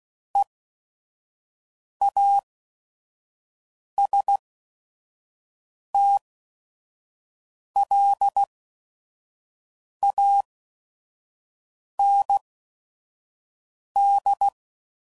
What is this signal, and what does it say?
Morse code, heard as a keyed tone, about 16 wpm, spaced as for 5 wpm, 800 Hz. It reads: EASTLAND
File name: CW Code Groups @5WPM / fromSM's / Cities